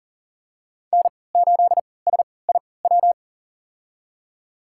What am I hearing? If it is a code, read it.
Morse code, heard as a keyed tone, about 40 wpm, spaced as for 23 wpm, 700 Hz. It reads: N8SIW
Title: USA Call Signs - 5 Characters - 3x Character Spacing - 40wpm